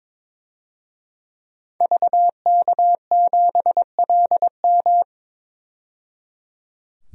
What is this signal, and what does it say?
Morse code, heard as a keyed tone, about 22 wpm, 700 Hz. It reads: VK7LM
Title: DXCC-100 Call Signs - 5 Characters - 22wpm